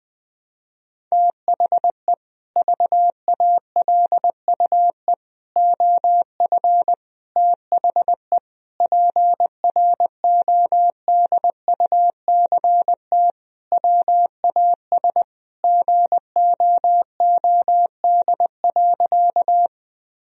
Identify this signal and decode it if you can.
Morse code, heard as a keyed tone, about 20 wpm, 700 Hz. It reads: THE VALUE OF THE PRODUCT WAS GOOD.
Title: Sentences from Top 1000 Words - Review Part 3 - 20wpm